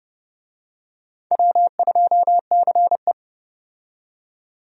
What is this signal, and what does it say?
Morse code, heard as a keyed tone, about 30 wpm, 700 Hz. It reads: W2CE
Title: USA Call Signs - 4 Characters - 30wpm